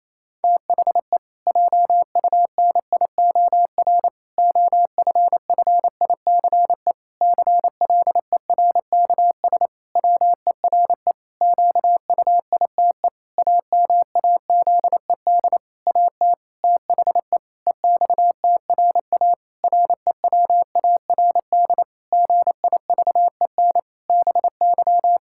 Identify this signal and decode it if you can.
Morse code, heard as a keyed tone, about 27 wpm, 700 Hz. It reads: THE JUNIOR OFFICE CLERKS WERE QUITE AMAZED AT THE EXTRA REWARD GIVEN BY
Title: Pangram Challenge v2 - 28wpm